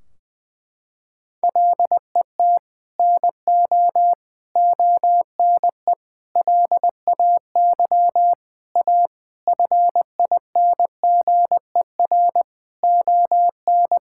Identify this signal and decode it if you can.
Morse code, heard as a keyed tone, about 21 wpm, 700 Hz. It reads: LET NO ONE LAY A FINGER ON
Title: Sentences from Top 1600 Words 20wpm